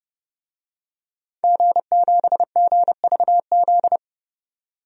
Morse code, heard as a keyed tone, about 30 wpm, 700 Hz. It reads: G7GVZ
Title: DXCC-100 Call Signs - 5 Characters - 30wpm